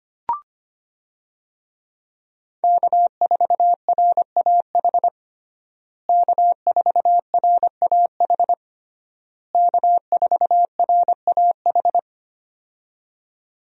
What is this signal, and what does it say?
Morse code, heard as a keyed tone, about 25 wpm, 700 Hz. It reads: K4RAH K4RAH K4RAH
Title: USA Call Signs - 5 Characters - Repeated 3x - 25wpm